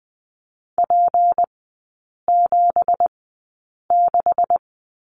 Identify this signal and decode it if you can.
Morse code, heard as a keyed tone, about 20 wpm, spaced as for 8 wpm, 690 Hz. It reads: P76
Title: 205 Lesson 33e - Letter j - Three Characters 20wpm